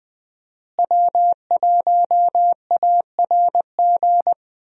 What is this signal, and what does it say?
Morse code, heard as a keyed tone, about 20 wpm, 700 Hz. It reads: W1ARG
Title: USA Call Signs - 5 Characters - Repeated 2x - 20wpm